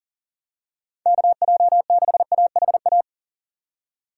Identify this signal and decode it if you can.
Morse code, heard as a keyed tone, about 40 wpm, 700 Hz. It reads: KJ6AHA
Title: USA Call Signs - 2x Character Spacing - 40wpm